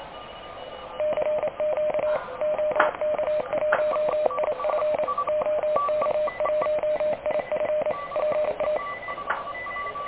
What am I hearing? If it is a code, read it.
Morse code, heard as a keyed tone, about 28 wpm, 610 Hz, with radio noise. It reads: CQ OK1IF OK1IF PA
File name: truSDX / KVPA ruSDXC